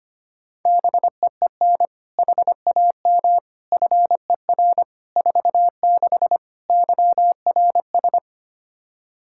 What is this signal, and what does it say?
Morse code, heard as a keyed tone, about 25 wpm, 700 Hz. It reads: BEEN HAM FER 46 YRS
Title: K1USN SST QSO Practice 25wpm